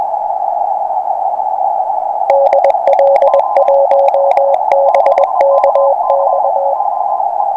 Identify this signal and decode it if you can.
Morse code, heard as a keyed tone, about 20 wpm, 560 Hz, with noise. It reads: DL1BKX
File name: attachments / 40m up dn